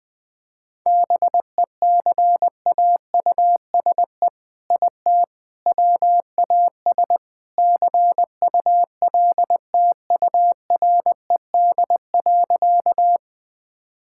Morse code, heard as a keyed tone, about 21 wpm, 700 Hz. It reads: BECAUSE IT WAS CULTURED.
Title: All-Age Appropriate Jokes - 20wpm